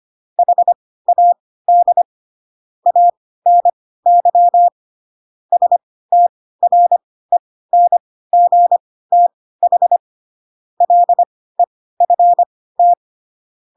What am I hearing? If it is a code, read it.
Morse code, heard as a keyed tone, about 25 wpm, spaced as for 16 wpm, 700 Hz. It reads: HAD ANY STRENGTH LEFT
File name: Pinocchio / Pinocchio ch6 10 25 ik1hge